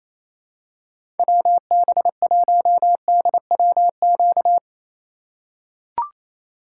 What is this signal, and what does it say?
Morse code, heard as a keyed tone, about 28 wpm, 700 Hz. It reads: WB1DWQ
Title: USA Call Signs - 6 Characters - 3x Character Spacing - 28wpm